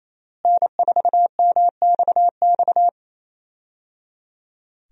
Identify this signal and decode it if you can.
Morse code, heard as a keyed tone, about 28 wpm, 700 Hz. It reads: N4MXX
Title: USA Call Signs - 5 Characters - Repeated 3x - 28wpm